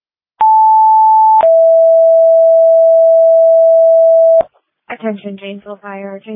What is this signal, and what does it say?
Two-tone page, heard as a one-second tone, then a three-second tone, about 880 then 660 Hz, with speech. It is heard from 0.4 to 4.4 s.